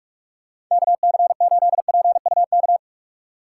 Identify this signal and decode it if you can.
Morse code, heard as a keyed tone, about 44 wpm, 700 Hz. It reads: KC8PUK
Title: USA Call Signs - 6 Characters - 8x Character Spacing - 45wpm